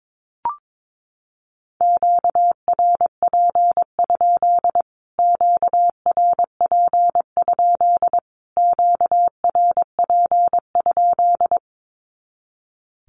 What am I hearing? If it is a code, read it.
Morse code, heard as a keyed tone, about 22 wpm, 700 Hz. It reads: QRP? QRP? QRP?
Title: Q-Codes Repeated 3x 22wpm